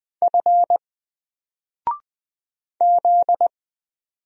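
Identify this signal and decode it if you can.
Morse code, heard as a keyed tone, about 20 wpm, 700 Hz. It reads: F Z
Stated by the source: Single letter and number Morse drill at 20 wpm, from N4LCD.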